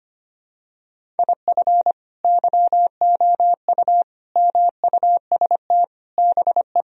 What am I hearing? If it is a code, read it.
Morse code, heard as a keyed tone, about 25 wpm, 700 Hz. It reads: IF YOU MUST BE
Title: Sentences from Top 1000 Words - Review Part 1 - 25wpm